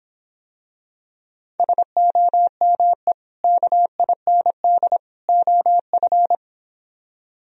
Morse code, heard as a keyed tone, about 26 wpm, 700 Hz. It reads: SOME KIND OF